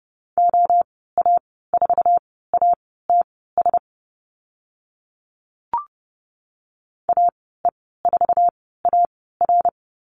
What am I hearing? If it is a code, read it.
Morse code, heard as a keyed tone, about 30 wpm, spaced as for 17 wpm, 700 Hz. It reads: OA4ATS AE4AR
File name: Lesson 14g - Number 2 - Call Signs 30wpm